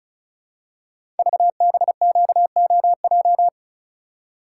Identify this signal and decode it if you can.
Morse code, heard as a keyed tone, about 36 wpm, 690 Hz. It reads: VBQOJ
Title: Code Groups - 5 Letters - 35wpm